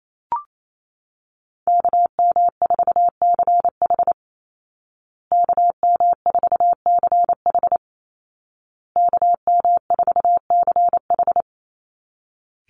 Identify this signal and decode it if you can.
Morse code, heard as a keyed tone, about 28 wpm, 700 Hz. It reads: KM4CH KM4CH KM4CH
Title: Call Signs Repeated 2x - 28wpm